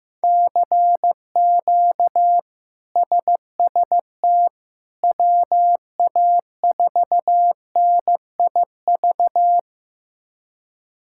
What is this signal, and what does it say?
Morse code, heard as a keyed tone, about 16 wpm, 700 Hz. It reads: CQ SST WA4NIV
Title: K1USN SST QSO Practice - 15wpm